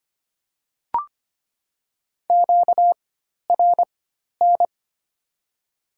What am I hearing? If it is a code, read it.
Morse code, heard as a keyed tone, about 25 wpm, spaced as for 12 wpm, 700 Hz. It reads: QRN
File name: Top 90 QSO Elements - 4x Character Spacing - 25wpm